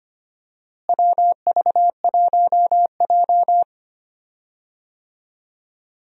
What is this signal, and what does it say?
Morse code, heard as a keyed tone, about 25 wpm, 700 Hz. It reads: WV1J
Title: USA Call Signs - 4 Characters - Repeated 2x - 25wpm